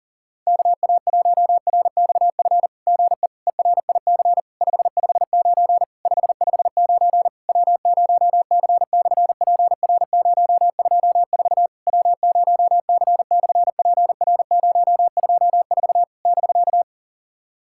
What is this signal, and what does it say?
Morse code, heard as a keyed tone, about 40 wpm, 700 Hz. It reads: KA1RXF GE ERIC 559 559 W0C/PR024 W0C/PR024 <BK>
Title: SOTA QSO Practice 40wpm